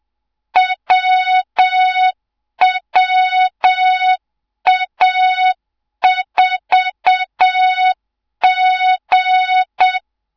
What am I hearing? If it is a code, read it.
Morse code, heard as a keyed tone, about 7 wpm, 3700 Hz, with noise. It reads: WWA4G
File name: Morze-HA2MN / hang / 24 GYAKORLAT